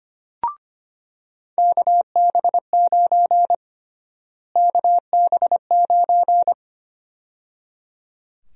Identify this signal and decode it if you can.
Morse code, heard as a keyed tone, about 25 wpm, 700 Hz. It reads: KB9 KB9
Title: USA Call Sign Prefixes - Repeated 2x - 25wpm